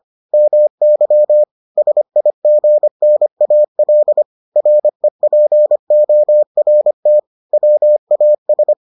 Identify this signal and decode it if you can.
Morse code, heard as a keyed tone, about 25 wpm, 600 Hz. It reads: MY SIGNAL REPORT WAS